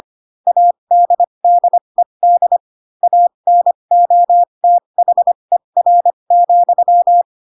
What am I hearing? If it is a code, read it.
Morse code, heard as a keyed tone, about 25 wpm, spaced as for 22 wpm, 700 Hz. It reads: ADDED ANOTHER,
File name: Pinocchio / Pinocchio ch3 18 25 ik1hge